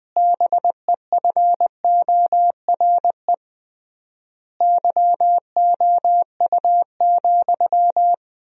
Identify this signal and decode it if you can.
Morse code, heard as a keyed tone, about 20 wpm, 700 Hz. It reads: BEFORE YOU,
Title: 3x Word-Spacing - Sentences from Top 400 Words - 20wpm